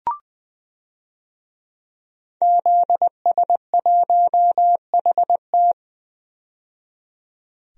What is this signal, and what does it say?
Morse code, heard as a keyed tone, about 21 wpm, 700 Hz. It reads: ZS1HT
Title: DXCC-100 Call Signs - 5 Characters - 20wpm